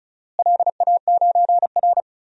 Morse code, heard as a keyed tone, about 34 wpm, 700 Hz. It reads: LA9R
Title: DXCC-100 Call Signs - 4 Characters - Repeated 2x - 35wpm